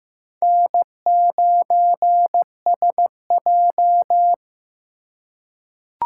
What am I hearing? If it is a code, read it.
Morse code, heard as a keyed tone, about 15 wpm, 700 Hz. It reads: N9SJ
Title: USA Call Signs - 3x Character Spacing - 15wpm